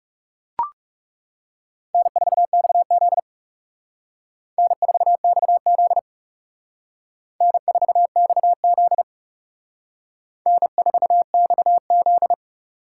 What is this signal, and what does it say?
Morse code, heard as a keyed tone, about 36 wpm, 700 Hz. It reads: N4XZ N4XZ N4XZ N4XZ
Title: Speed Racing - USA Call Signs 30wpm